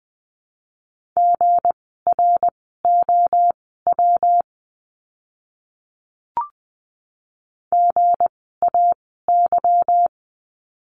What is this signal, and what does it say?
Morse code, heard as a keyed tone, about 20 wpm, spaced as for 15 wpm, 700 Hz. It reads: GROW GAY
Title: 160 Lesson 26f - Letter g - Words 20wpm